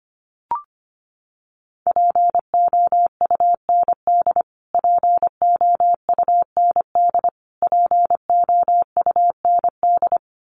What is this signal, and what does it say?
Morse code, heard as a keyed tone, about 25 wpm, 700 Hz. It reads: POUND POUND POUND
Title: Top 500 Words Repeated 25wpm